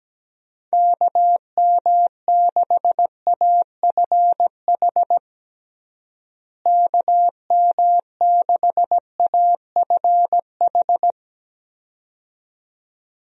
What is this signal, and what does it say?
Morse code, heard as a keyed tone, about 17 wpm, 700 Hz. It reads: KM6AFH KM6AFH
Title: USA Call Signs - 6 Characters - Repeated 2x - 17wpm